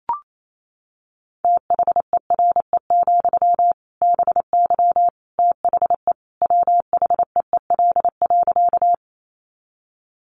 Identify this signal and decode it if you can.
Morse code, heard as a keyed tone, about 28 wpm, 700 Hz. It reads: THERE, BY THE WHEEL.
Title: Sentences from Top 500 Words 28wpm